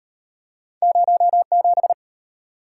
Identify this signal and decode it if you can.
Morse code, heard as a keyed tone, about 34 wpm, 700 Hz. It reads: 07